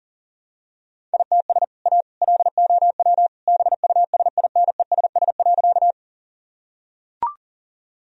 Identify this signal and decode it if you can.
Morse code, heard as a keyed tone, about 40 wpm, 700 Hz. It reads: ITS A LOW BUSINESS.
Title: Sentences from Top 1000 Words - Review Part 1 - 40wpm